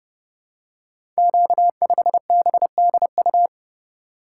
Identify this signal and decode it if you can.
Morse code, heard as a keyed tone, about 30 wpm, 700 Hz. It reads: Q5BDU
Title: Code Groups - 5 Letter:Numbers - 30wpm